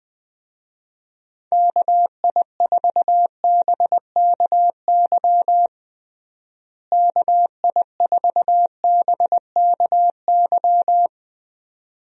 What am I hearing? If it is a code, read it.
Morse code, heard as a keyed tone, about 20 wpm, 700 Hz. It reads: KI4BKY KI4BKY
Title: USA Call Signs - 6 Characters - Repeated 2x - 20wpm